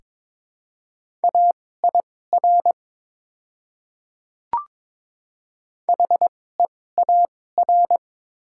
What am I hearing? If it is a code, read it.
Morse code, heard as a keyed tone, about 23 wpm, spaced as for 16 wpm, 700 Hz. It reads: AIR HEAR
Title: 2x Character Spacing - Top 500 Words - 22wpm